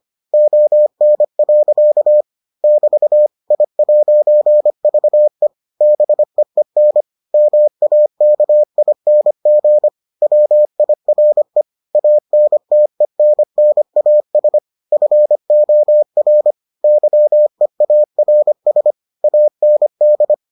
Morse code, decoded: ON. <BT> I'VE BEEN MAKING WIRE ANTENNAS FOR YEARS AND

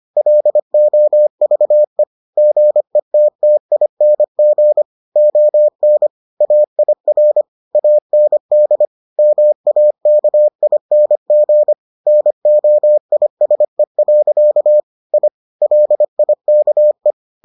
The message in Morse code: LOVE GETTING ON AIR AND MAKING NOISE. I LIKE

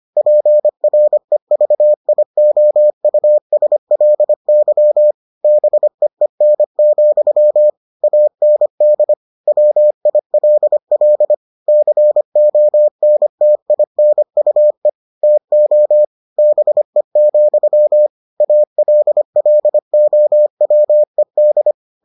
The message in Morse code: PREVIOUSLY BEEN, AND WILL CONTINUE TO BE, ALLOWED

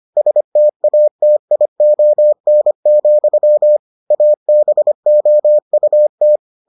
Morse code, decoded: STATION, ABOUT

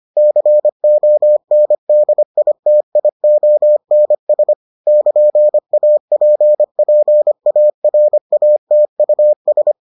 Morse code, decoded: CONDITIONS <KN>APPARATUS

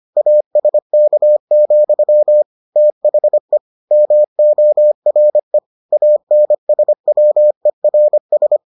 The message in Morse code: ASK, THE MORE ANSWERS

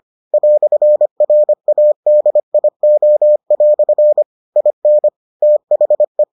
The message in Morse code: "RADIO" IN THE

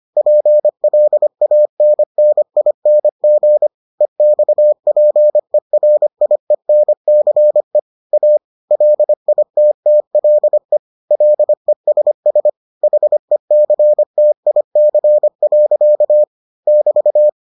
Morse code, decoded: PLANNING EXPERIENCE A LITTLE LESS HECTIC. <BT>

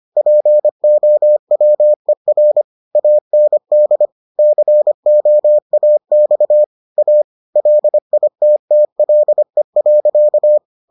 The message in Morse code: POWER AND COAX A LITTLE.

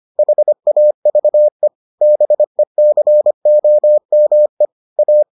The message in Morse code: HAVE BECOME A

